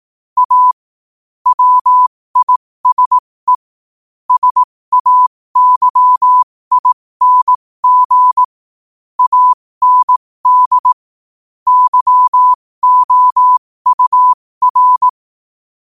Morse code, heard as a keyed tone, about 18 wpm, spaced as for 15 wpm, 1000 Hz. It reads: A WISE SAYING AND YOUR
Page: New quotes every day in morse code at 15 Words per minute.